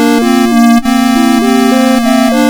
There is no code.